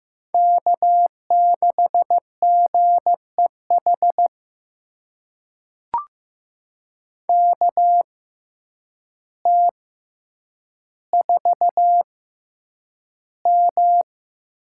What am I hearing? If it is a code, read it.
Morse code, heard as a keyed tone, about 15 wpm, spaced as for 8 wpm, 700 Hz. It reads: K6GEH KT4M
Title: USA Call Signs - 5 Characters - 6x Character Spacing - 15wpm